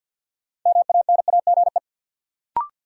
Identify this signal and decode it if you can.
Morse code, heard as a keyed tone, about 47 wpm, 700 Hz. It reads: MANAGE
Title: Sentences from Top 1400 Words - New Word Drill Alt - 50wpm